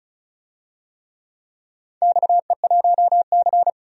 Morse code, heard as a keyed tone, about 36 wpm, 700 Hz. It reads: XE1C
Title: DXCC-100 Call Signs - 4 Characters - 35wpm